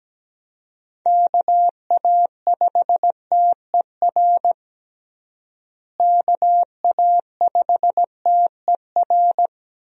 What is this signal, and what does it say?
Morse code, heard as a keyed tone, about 17 wpm, 700 Hz. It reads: KA5TER KA5TER
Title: USA Call Signs - 6 Characters - Repeated 3x - 17wpm